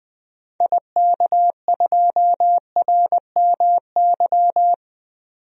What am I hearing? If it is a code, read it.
Morse code, heard as a keyed tone, about 20 wpm, 700 Hz. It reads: IK2RMY
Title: DXCC-100 Call Signs - 6 Characters - 20wpm